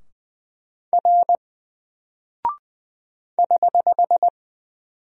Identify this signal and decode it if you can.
Morse code, heard as a keyed tone, about 20 wpm, 700 Hz. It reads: R <HH>
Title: Single Character (Letter:Number:Punctuation:Prosign) 20wpm